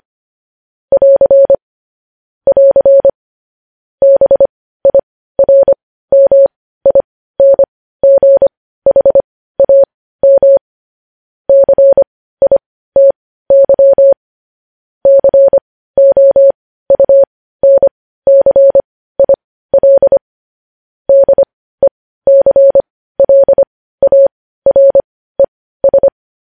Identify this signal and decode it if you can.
Morse code, heard as a keyed tone, about 25 wpm, spaced as for 15 wpm, 550 Hz. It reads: <AR> <AR> BIRMINGHAM CITY COUNCIL DECLARES
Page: News Headlines in Morse Code at 15 WPM